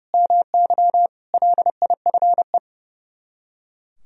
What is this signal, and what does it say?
Morse code, heard as a keyed tone, about 37 wpm, 700 Hz. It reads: MY LIFE
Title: Speed-Racing - Sets of 5 Words 30wpm